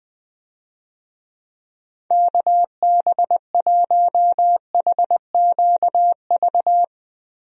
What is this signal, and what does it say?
Morse code, heard as a keyed tone, about 19 wpm, 700 Hz. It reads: KB1HQV
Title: USA Call Signs - 6 Characters - Repeated 2x - 20wpm